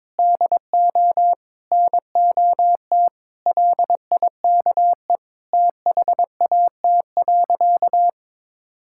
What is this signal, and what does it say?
Morse code, heard as a keyed tone, about 22 wpm, 700 Hz. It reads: DO NOT LIKE THAT.
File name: practice / 22wpm / Sentences with 3-Word Phrases - 22wpm